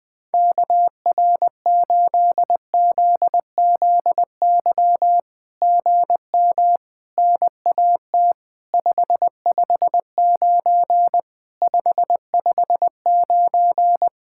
Morse code, heard as a keyed tone, about 20 wpm, 700 Hz. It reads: KR8ZZY GM NAT 559 559
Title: SOTA QSO Practice 20wpm